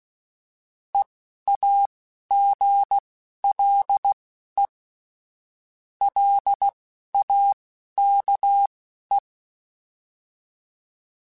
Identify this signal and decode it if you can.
Morse code, heard as a keyed tone, about 16 wpm, spaced as for 11 wpm, 800 Hz. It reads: EAGLE LAKE